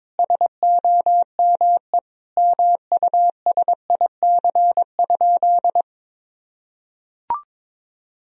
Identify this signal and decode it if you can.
Morse code, heard as a keyed tone, about 22 wpm, 700 Hz. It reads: SOME MUSIC?
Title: Sentences from Top 1000 Words - Review Part 1 - 22wpm